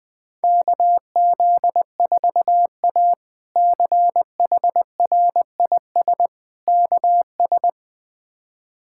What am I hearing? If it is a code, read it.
Morse code, heard as a keyed tone, about 20 wpm, 700 Hz. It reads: KZ4A CHRIS KS
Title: K1USN SST QSO Practice - 20wpm